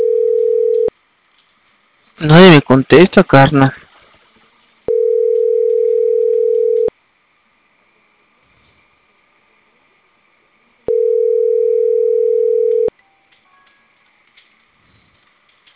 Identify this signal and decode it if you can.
Ringback tone (North America), 440 + 480 Hz, with speech.
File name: CENTRAL CAEA / out agcaea02 - .47454